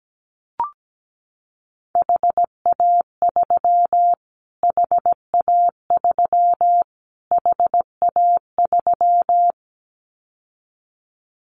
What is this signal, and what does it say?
Morse code, heard as a keyed tone, about 17 wpm, 700 Hz. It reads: HA3 HA3 HA3
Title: Callsign Prefixes for Top 25 DXCC - Repeated 3x - 17wpm